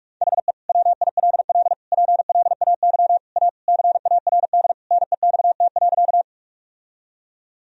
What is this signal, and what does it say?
Morse code, heard as a keyed tone, about 46 wpm, 700 Hz. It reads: HE WILL PLAY A CARD NEXT.
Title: Sentences from Top 1000 Words - Review Part 4 - 45wpm